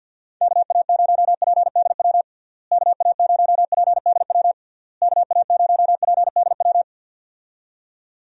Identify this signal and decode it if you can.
Morse code, heard as a keyed tone, about 47 wpm, 700 Hz. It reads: KA0PDW KA0PDW KA0PDW
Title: USA Call Signs - 6 Characters - Repeated 3x - 50wpm